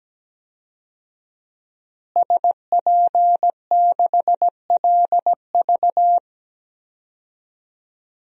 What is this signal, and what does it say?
Morse code, heard as a keyed tone, about 17 wpm, 700 Hz. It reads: SP6LV
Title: DXCC-100 Call Signs - 5 Characters - Repeated 2x - 17wpm